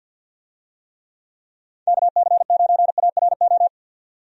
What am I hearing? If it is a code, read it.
Morse code, heard as a keyed tone, about 52 wpm, 700 Hz. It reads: KC9ARO
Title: USA Call Signs - 6 Characters - 50wpm